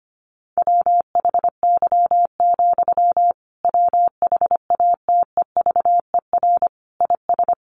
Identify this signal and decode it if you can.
Morse code, heard as a keyed tone, about 25 wpm, 700 Hz. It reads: WHY, WHATEVER IS
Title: Sentences from Top 1100 Words 25wpm